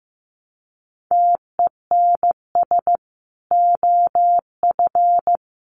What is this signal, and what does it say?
Morse code, heard as a keyed tone, about 15 wpm, 700 Hz. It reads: TENS OF
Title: Sets of 5 Words Encore2 15wpm